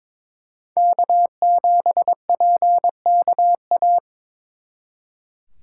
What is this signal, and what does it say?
Morse code, heard as a keyed tone, about 26 wpm, 700 Hz. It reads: K7PKA
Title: Speed Racing - USA 1x3 Call Signs 22wpm